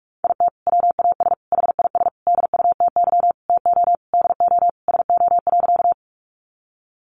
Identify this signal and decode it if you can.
Morse code, heard as a keyed tone, about 44 wpm, 700 Hz. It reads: IT WAS HIS DUTY TO DO SO.
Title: Sentences from Top 1500 Words 45wpm